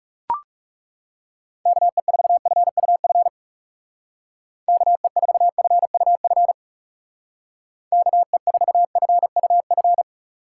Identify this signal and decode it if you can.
Morse code, heard as a keyed tone, about 35 wpm, 700 Hz. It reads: KE4FUF KE4FUF KE4FUF
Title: Speed Racing - USA Call Signs 30wpm